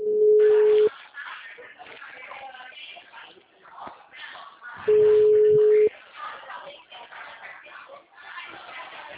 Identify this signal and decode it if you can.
Ringback tone (Europe, 425 Hz), with speech.
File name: out- -2010- - - .56338